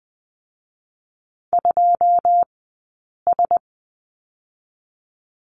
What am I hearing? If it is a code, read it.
Morse code, heard as a keyed tone, about 19 wpm, 700 Hz. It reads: 2 S
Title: 217 Lesson 35d - Number 0 - Two Characters 20wpm